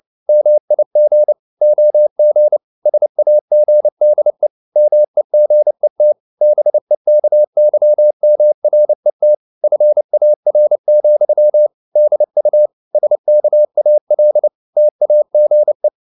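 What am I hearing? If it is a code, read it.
Morse code, heard as a keyed tone, about 29 wpm, 600 Hz. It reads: MIG OG SAGDE MEGET BEKYMRET FAR, DU SKAL TAGE
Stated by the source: Fast (QRQ) Morse at 29 wpm, from CW med Gnister.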